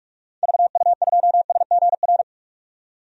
Morse code, heard as a keyed tone, about 46 wpm, 700 Hz. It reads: VU2SGR
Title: DXCC-100 Call Signs - 6 Characters - 45wpm